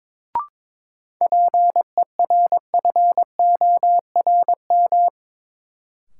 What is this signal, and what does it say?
Morse code, heard as a keyed tone, about 22 wpm, 700 Hz. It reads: PERFORM
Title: Sentences from Top 1000 Words - Review Part 5 - New Word Drill - 22wpm